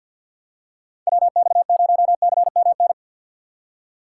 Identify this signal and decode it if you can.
Morse code, heard as a keyed tone, about 52 wpm, 700 Hz. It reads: WX0CMN
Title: USA Call Signs - 4x Character Spacing - 50wpm